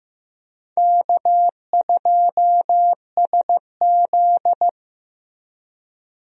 Morse code, heard as a keyed tone, about 15 wpm, 700 Hz. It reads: K2SZ
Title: USA Call Signs - 4 Characters - Repeated 2x - 15wpm